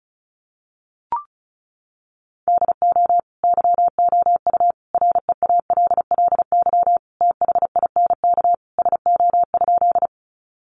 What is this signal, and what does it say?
Morse code, heard as a keyed tone, about 34 wpm, 700 Hz. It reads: DO YOU REALLY THINK SO?
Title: Sentences from Top 600 Words 35wpm